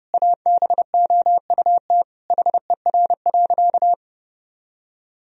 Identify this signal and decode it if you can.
Morse code, heard as a keyed tone, about 32 wpm, 700 Hz. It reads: ABOUT HER.
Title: Sentences from Top 1000 Words - Review Part 4 - 30wpm